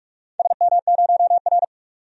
Morse code, heard as a keyed tone, about 44 wpm, 700 Hz. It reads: SM0R
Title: DXCC-100 Call Signs - 4 Characters - 45wpm